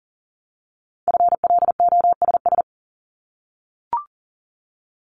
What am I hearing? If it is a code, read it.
Morse code, heard as a keyed tone, about 40 wpm, 700 Hz. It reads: FLOSS
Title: Multi-letter Phonemes Repeated 3x 40wpm